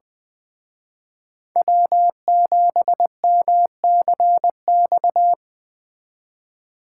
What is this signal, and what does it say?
Morse code, heard as a keyed tone, about 20 wpm, 700 Hz. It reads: W7MCX